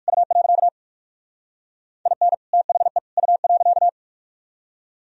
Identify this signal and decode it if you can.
Morse code, heard as a keyed tone, about 56 wpm, 700 Hz. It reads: U. IN THE U.
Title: Speed-Racing - Sets of 3 Words 45wpm